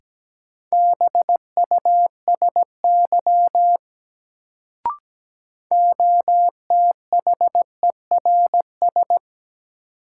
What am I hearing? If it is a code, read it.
Morse code, heard as a keyed tone, about 17 wpm, 700 Hz. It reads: BUSY OTHERS
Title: Sentences from Top 1000 Words - Review Part 2 - New Word Drill - 17wpm